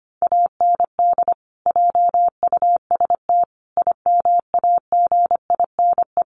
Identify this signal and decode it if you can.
Morse code, heard as a keyed tone, about 25 wpm, 700 Hz. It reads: AND JUST IMAGINE
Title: Sentences from Top 1400 Words 25wpm